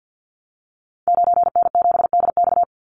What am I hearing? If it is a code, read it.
Morse code, heard as a keyed tone, about 48 wpm, 700 Hz. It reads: 9N7DX
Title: Call Signs Repeated 3x - 50wpm